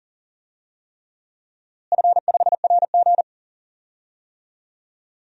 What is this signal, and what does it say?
Morse code, heard as a keyed tone, about 40 wpm, 700 Hz. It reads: F5RG
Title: DXCC-100 Call Signs - 4 Characters - Repeated 3x - 40wpm